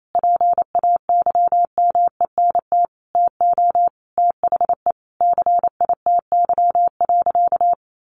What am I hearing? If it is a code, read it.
Morse code, heard as a keyed tone, about 28 wpm, 700 Hz. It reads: PAYMENT TO THE CITY.
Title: Sentences from Top 1400 Words 28wpm